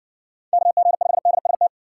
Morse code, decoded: KG5DST

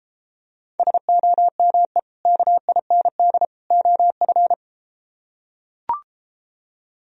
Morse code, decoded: SOME KIND OF

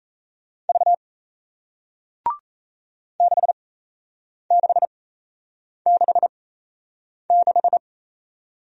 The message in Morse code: V 6666

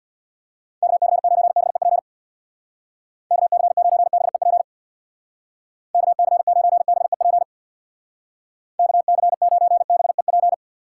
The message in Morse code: KC9BEP KC9BEP KC9BEP KC9BEP

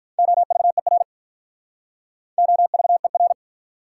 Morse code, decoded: OVER OVER

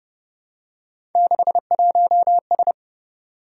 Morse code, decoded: 61S

